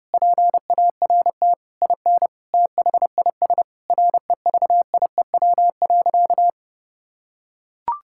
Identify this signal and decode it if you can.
Morse code, heard as a keyed tone, about 30 wpm, 700 Hz. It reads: PART IN THIS REVIEW.
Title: Sentences from Top 1000 Words - Review Part 6 - 30wpm